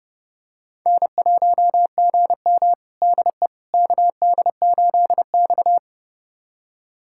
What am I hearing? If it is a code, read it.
Morse code, heard as a keyed tone, about 30 wpm, 700 Hz. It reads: N1GM DE KD8X
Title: USA Callsign DE Callsign - 4 Characters - 30wpm